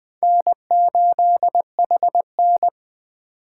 Morse code, heard as a keyed tone, about 21 wpm, 700 Hz. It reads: N8HN